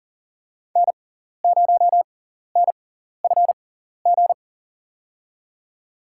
Morse code, heard as a keyed tone, about 40 wpm, spaced as for 14 wpm, 700 Hz. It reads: N0NFG
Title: USA Call Signs - 5 Characters - 6x Character Spacing - 40wpm